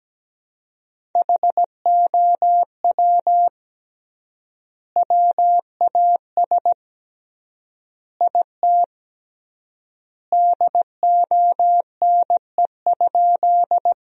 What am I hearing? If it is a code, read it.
Morse code, heard as a keyed tone, about 17 wpm, 700 Hz. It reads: HOW WAS IT DONE?
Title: 3x Word-Spacing - Sentences from Top 500 Words - 17wpm